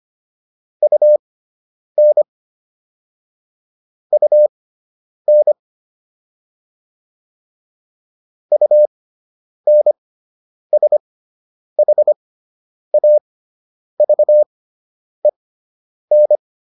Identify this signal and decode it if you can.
Morse code, heard as a keyed tone, about 25 wpm, spaced as for 9 wpm, 600 Hz. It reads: UN UN UNSHAVEN